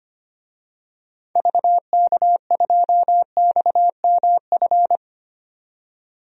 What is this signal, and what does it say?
Morse code, heard as a keyed tone, about 25 wpm, 690 Hz. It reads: VK2XMF
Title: DXCC-100 Call Signs - 6 Characters - 25wpm